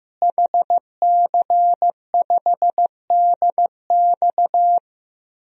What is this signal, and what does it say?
Morse code, heard as a keyed tone, about 15 wpm, 700 Hz. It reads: HC5DX
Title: DXCC-100 Call Signs - 5 Characters - Repeated 3x - 15wpm